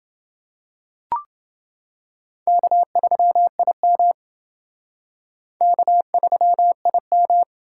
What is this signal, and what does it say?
Morse code, heard as a keyed tone, about 29 wpm, 700 Hz. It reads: K3IM K3IM